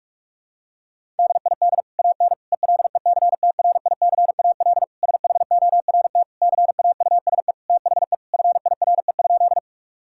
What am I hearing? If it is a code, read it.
Morse code, heard as a keyed tone, about 46 wpm, 700 Hz. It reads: DID AN ELECTRICAL SHORT CAUSE THE FIRE?
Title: Sentences from Top 1000 Words - Review Part 6 - 45wpm